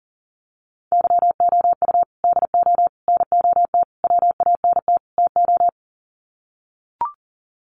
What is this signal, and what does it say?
Morse code, heard as a keyed tone, about 40 wpm, 700 Hz. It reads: YOU DO NOT WANT TO
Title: Sets of 5 Words 40wpm